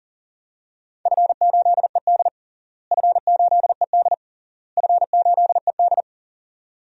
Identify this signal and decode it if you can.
Morse code, heard as a keyed tone, about 40 wpm, 700 Hz. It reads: F8ED F8ED F8ED
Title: DXCC-100 Call Signs - 4 Characters - Repeated 3x - 40wpm